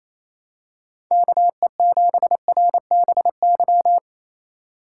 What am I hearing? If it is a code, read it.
Morse code, heard as a keyed tone, about 27 wpm, 700 Hz. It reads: KE7RBY
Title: USA Call Signs - 4x Character Spacing - 28wpm